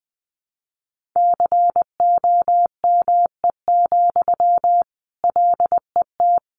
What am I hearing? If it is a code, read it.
Morse code, heard as a keyed tone, about 20 wpm, 700 Hz. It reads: COME, LET
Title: Sentences from Top 300 Words 20wpm